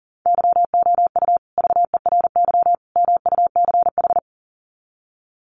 Morse code, heard as a keyed tone, about 40 wpm, 700 Hz. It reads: YOU VERY MUCH
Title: Sets of 3 Words 40wpm